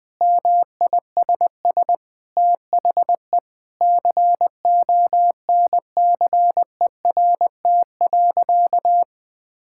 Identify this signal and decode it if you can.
Morse code, heard as a keyed tone, about 20 wpm, 700 Hz. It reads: MISS THE CONCERT.
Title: Sentences with 8-Word Phrases - 20wpm